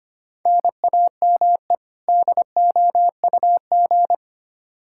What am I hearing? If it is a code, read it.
Morse code, heard as a keyed tone, about 25 wpm, 700 Hz. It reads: NAME DOUG
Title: Practice QSO - Exchange 1 - Beginning - 25wpm